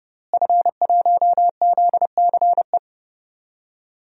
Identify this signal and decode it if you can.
Morse code, heard as a keyed tone, about 30 wpm, 700 Hz. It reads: F1ZCE
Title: DXCC-100 Call Signs - 5 Characters - Repeated 2x - 30wpm